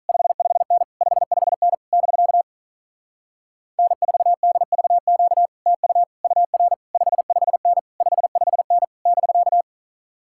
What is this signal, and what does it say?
Morse code, heard as a keyed tone, about 47 wpm, 700 Hz. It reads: 55N 55N <BK> N4DVQ TU UR 55N 55N <BK>